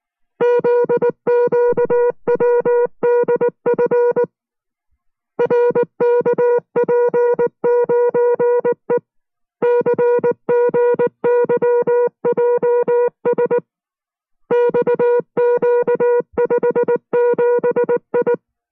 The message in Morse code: ZQWDF RKP9E CGYJS XQ57I